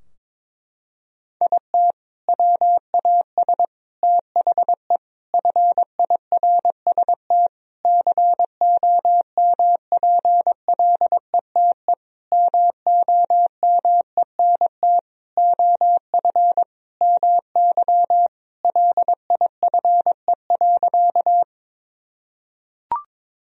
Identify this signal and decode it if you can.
Morse code, heard as a keyed tone, about 22 wpm, 700 Hz. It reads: IT WAS THE FIRST COMPLETE MOMENT OF MY LIFE.
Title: Sentences from Top 1000 Words - Review Part 2 - 22wpm